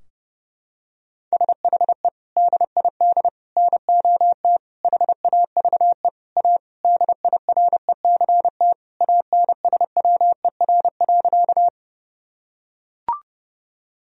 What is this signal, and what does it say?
Morse code, heard as a keyed tone, about 30 wpm, 700 Hz. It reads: SHE DID NOT HAVE A DIRECT ANSWER.
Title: Sentences from Top 1000 Words - Review Part 2 - 30wpm